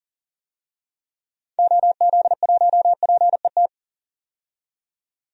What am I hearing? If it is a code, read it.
Morse code, heard as a keyed tone, about 40 wpm, 700 Hz. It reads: OZ1PET
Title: DXCC-100 Call Signs - 6 Characters - 40wpm